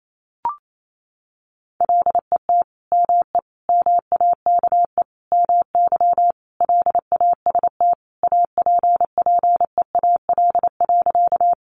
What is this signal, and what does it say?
Morse code, heard as a keyed tone, about 28 wpm, 700 Hz. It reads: LET ME MAKE MY LAST APPEAL.
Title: Sentences from Top 1300 Words 28wpm